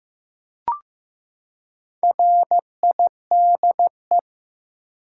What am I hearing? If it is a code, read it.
Morse code, heard as a keyed tone, about 15 wpm, 700 Hz. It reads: RIDE
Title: Four Letter Words - 15wpm